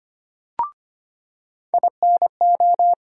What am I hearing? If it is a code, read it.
Morse code, heard as a keyed tone, about 25 wpm, 700 Hz. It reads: INO
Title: USA Call Sign Postfixes - 3 Letters 25wpm